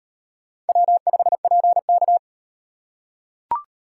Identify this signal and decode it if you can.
Morse code, heard as a keyed tone, about 35 wpm, 700 Hz. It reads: W5PK